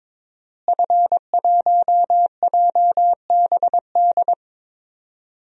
Morse code, heard as a keyed tone, about 22 wpm, 700 Hz. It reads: F1JBD
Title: DXCC-100 Call Signs - 5 Characters - Repeated 3x - 22wpm